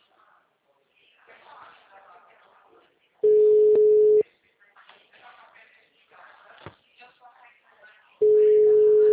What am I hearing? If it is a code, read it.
Ringback tone (Europe, 425 Hz), with speech.